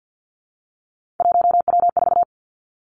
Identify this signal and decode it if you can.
Morse code, heard as a keyed tone, about 50 wpm, 700 Hz. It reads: 1W4
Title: Three Challenging Letter-Number Combinations 50wpm